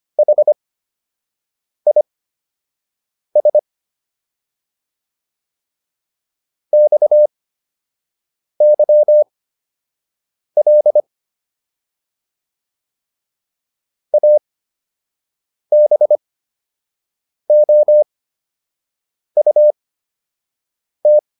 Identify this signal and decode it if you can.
Morse code, heard as a keyed tone, about 25 wpm, spaced as for 6 wpm, 600 Hz. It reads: HIS XYL ABOUT